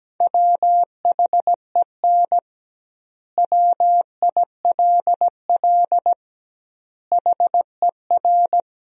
Morse code, heard as a keyed tone, about 17 wpm, 700 Hz. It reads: WHEN WILL HER
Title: 2x Word-Spacing - Sentences from Top 200 Words - 17wpm